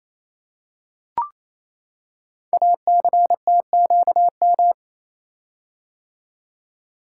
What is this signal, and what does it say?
Morse code, heard as a keyed tone, about 27 wpm, 700 Hz. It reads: ACTQM
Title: Code Groups - 5 Letters - 28wpm